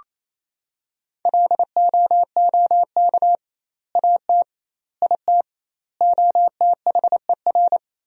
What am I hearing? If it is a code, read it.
Morse code, heard as a keyed tone, about 28 wpm, 700 Hz. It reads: LOOK AT IT OTHER
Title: 2x Word-Spacing - Sentences from Top 200 Words - 28wpm